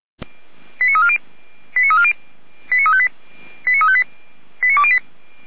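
Five-tone selective call (ZVEI), heard as quick runs of five tones, tones of about 70 ms. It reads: 89340 89358 89358 89208
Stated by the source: ZVEI five-tone selective call 89304 received by SDR.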